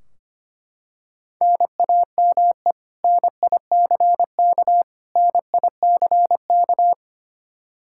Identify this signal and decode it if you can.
Morse code, heard as a keyed tone, about 25 wpm, 700 Hz. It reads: NAME NICK NICK
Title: Practice QSO - Exchange 1 - Beginning - 25wpm